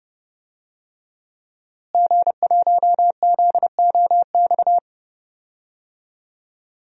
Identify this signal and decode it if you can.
Morse code, heard as a keyed tone, about 30 wpm, 700 Hz. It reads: G1ZOX
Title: DXCC-100 Call Signs - 5 Characters - Repeated 3x - 30wpm